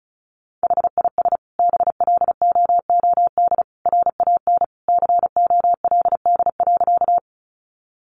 Morse code, heard as a keyed tone, about 34 wpm, 700 Hz. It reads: HIS BLOOD RAN COLD.
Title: Sentences from Top 1100 Words 35wpm